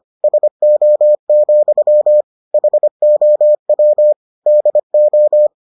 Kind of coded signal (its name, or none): Morse code